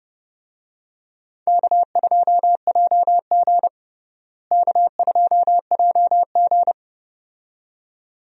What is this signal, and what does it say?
Morse code, heard as a keyed tone, about 31 wpm, 700 Hz. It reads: K2JG K2JG
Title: USA Call Signs - 4 Characters - Repeated 2x - 30wpm